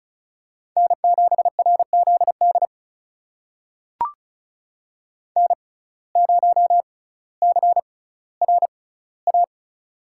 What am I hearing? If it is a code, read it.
Morse code, heard as a keyed tone, about 36 wpm, spaced as for 18 wpm, 700 Hz. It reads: N7RZD N0CRA
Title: USA Call Signs - 5 Characters - 6x Character Spacing - 35wpm